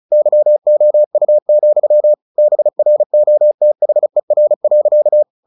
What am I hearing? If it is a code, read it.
Morse code, heard as a keyed tone, about 34 wpm, 600 Hz. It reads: YOU, BROTHER.
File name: the old man and the sea 2 35wpm